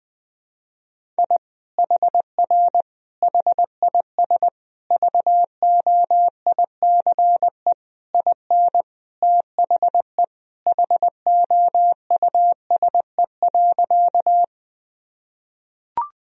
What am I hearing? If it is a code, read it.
Morse code, heard as a keyed tone, about 20 wpm, 700 Hz. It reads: I HR HIS VOICE IN THE HOUSE.
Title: Sentences from Top 500 Words and Abbreviations 20wpm